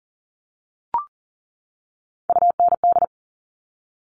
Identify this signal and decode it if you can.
Morse code, heard as a keyed tone, about 41 wpm, 690 Hz. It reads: UND
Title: Top Three Letter Combinations in Words 40wpm